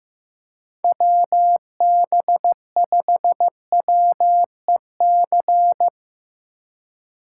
Morse code, heard as a keyed tone, about 16 wpm, 700 Hz. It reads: WB5WEC
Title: K1USN SST QSO Practice - 15wpm